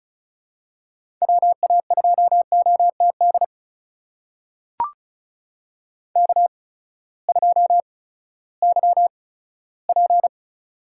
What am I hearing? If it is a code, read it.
Morse code, heard as a keyed tone, about 36 wpm, spaced as for 18 wpm, 700 Hz. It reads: WA2OTD K2YP
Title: USA Call Signs - 8x Character Spacing - 35wpm